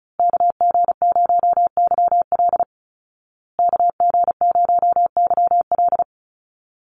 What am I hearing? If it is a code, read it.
Morse code, heard as a keyed tone, about 34 wpm, 700 Hz. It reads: KG0YL KG0YL
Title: Call Signs Repeated 2x - 35wpm